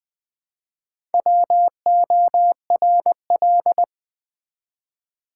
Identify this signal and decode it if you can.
Morse code, heard as a keyed tone, about 20 wpm, 700 Hz. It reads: WORL
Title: Build A Word - 1 Letter At A Time - 5-Letter Words - 20wpm